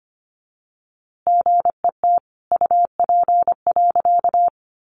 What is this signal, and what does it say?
Morse code, heard as a keyed tone, about 25 wpm, 700 Hz. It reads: GET UP.
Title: A Sound Of Thunder 25wpm